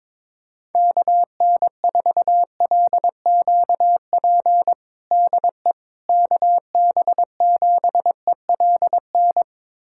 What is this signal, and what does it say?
Morse code, heard as a keyed tone, about 22 wpm, 700 Hz. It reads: KN4LQP DE KB7ELN
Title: USA Callsign DE Callsign 22wpm